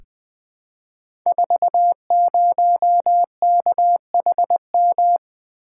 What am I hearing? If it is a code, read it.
Morse code, heard as a keyed tone, about 21 wpm, 700 Hz. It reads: 40KHM
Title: Code Groups - 5 Letter:Numbers - 20wpm